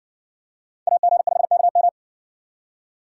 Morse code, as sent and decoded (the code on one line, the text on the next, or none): .- --. ..... -.- --
AG5KM